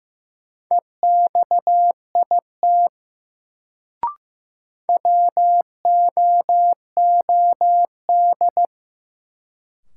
Morse code, heard as a keyed tone, about 15 wpm, 700 Hz. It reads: EXIT WOOD
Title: Sentences from Top 1100 Words - New Word Drill Alt - 15wpm